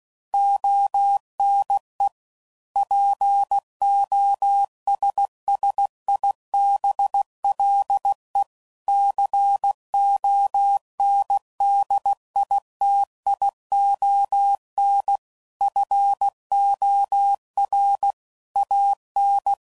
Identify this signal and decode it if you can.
Morse code, heard as a keyed tone, about 16 wpm, 800 Hz. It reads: ONE POSSIBLE CONDITION FOR AN
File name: CW Code Groups @15WPM / Technical Subjects / AND Gate